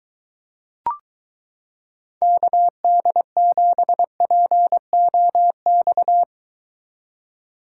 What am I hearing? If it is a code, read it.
Morse code, heard as a keyed tone, about 22 wpm, 700 Hz. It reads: KD7POX